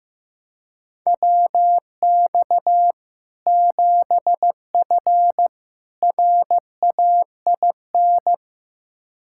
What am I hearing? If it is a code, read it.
Morse code, heard as a keyed tone, about 15 wpm, 700 Hz. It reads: WX 7F RAIN
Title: K1USN SST QSO Practice 15wpm